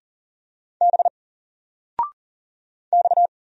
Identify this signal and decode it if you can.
Morse code, heard as a keyed tone, about 39 wpm, 700 Hz. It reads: B X
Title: Single Character (Letter:Number:Punctuation:Prosign) 40wpm